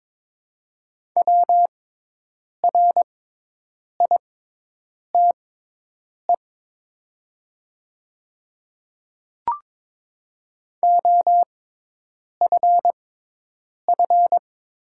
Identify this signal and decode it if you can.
Morse code, heard as a keyed tone, about 23 wpm, spaced as for 8 wpm, 700 Hz. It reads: WRITE OFF
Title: 6x Character Spacing - Top 200 Words - 22wpm